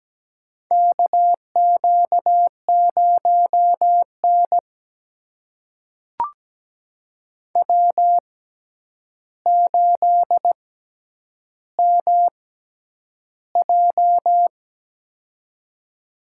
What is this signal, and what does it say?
Morse code, heard as a keyed tone, about 17 wpm, spaced as for 9 wpm, 700 Hz. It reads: KQ0N W8MJ
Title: USA Call Signs - 4 Characters - 6x Character Spacing - 17wpm